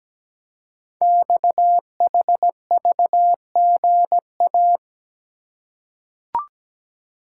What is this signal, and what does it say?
Morse code, heard as a keyed tone, about 17 wpm, 700 Hz. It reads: XHVGA
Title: Code Groups - 5 Letters - 17wpm